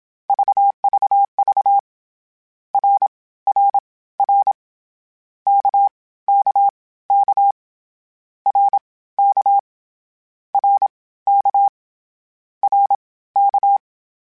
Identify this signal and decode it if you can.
Morse code, heard as a keyed tone, about 27 wpm, spaced as for 17 wpm, 800 Hz. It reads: VVV RRR KKK RK RK RK
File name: Short Runs / Short Runs-mp3s / Balanced @25